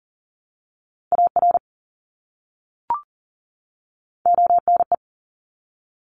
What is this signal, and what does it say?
Morse code, heard as a keyed tone, about 42 wpm, 700 Hz. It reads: AR ONE
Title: ARRL Sections 40wpm